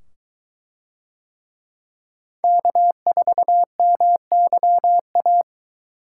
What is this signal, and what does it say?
Morse code, heard as a keyed tone, about 23 wpm, 700 Hz. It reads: K4MYA